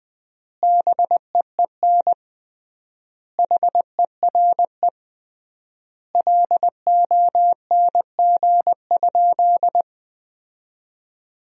Morse code, heard as a keyed tone, about 20 wpm, 700 Hz. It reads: BEEN HERE LONG?
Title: 3x Word-Spacing - Sentences from Top 200 Words - 20wpm